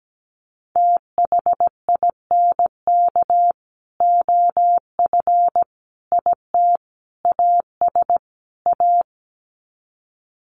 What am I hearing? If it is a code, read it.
Morse code, heard as a keyed tone, about 17 wpm, 700 Hz. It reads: THINK OF IT AS A
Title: Sets of 5 Words Encore2 17wpm